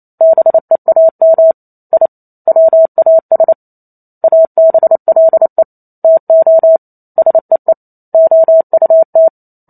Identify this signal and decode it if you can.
Morse code, heard as a keyed tone, about 28 wpm, 650 Hz. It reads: BEAM I WAS ABLE TO SEE OUT